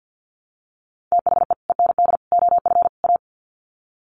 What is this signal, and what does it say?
Morse code, heard as a keyed tone, about 50 wpm, 700 Hz. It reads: THE END OF A
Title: Sets of 4 Words 50wpm